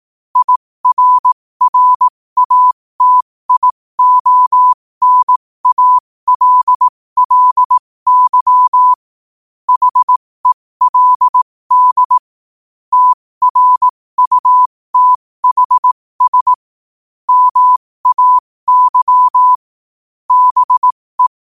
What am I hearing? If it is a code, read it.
Morse code, heard as a keyed tone, about 18 wpm, spaced as for 15 wpm, 1000 Hz. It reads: IRRATIONALLY HELD TRUTHS MAY BE
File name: qotdfiles / 15wpm